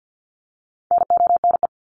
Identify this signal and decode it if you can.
Morse code, heard as a keyed tone, about 50 wpm, 700 Hz. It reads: NONE
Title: Top 1501-1750 Words 50wpm